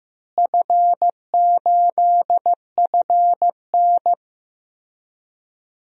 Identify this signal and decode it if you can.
Morse code, heard as a keyed tone, about 15 wpm, 700 Hz. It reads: F8FN
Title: DXCC-100 Call Signs - 4 Characters - Repeated 2x - 15wpm